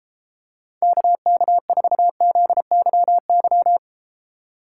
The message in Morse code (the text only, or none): KK4ZYY